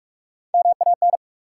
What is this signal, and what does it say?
Morse code, heard as a keyed tone, about 46 wpm, 700 Hz. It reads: MAN
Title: Sentences from Top 200 Words - New Word Drill - 45wpm